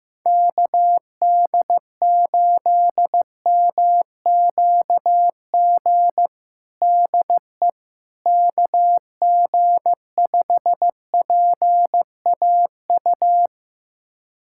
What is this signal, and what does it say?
Morse code, heard as a keyed tone, about 15 wpm, 700 Hz. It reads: KD8MQG DE KG5PAU
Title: USA Callsign DE Callsign 15wpm